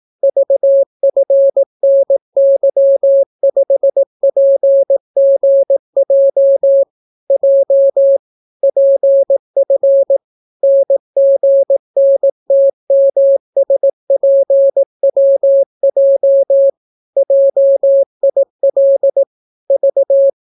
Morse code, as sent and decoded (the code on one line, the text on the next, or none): ...- ..-. -. -.-- ..... .--. --. .--- / .--- / .--. ..-. / -. --. -. - -- ... .--. .-- .--- / .--- .. .-.. / ...-
VFNY5PGJ J PF NGNTMSPWJ JIL V